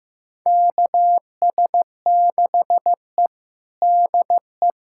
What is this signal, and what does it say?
Morse code, heard as a keyed tone, about 16 wpm, 700 Hz. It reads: KS6E DE
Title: USA Callsign DE Callsign - 4 Characters - 15wpm